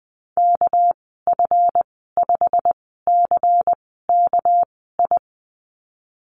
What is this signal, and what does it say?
Morse code, heard as a keyed tone, about 20 wpm, spaced as for 14 wpm, 700 Hz. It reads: KF5CKI
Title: 200 Lesson 32g - Letter k - Call Signs 20wpm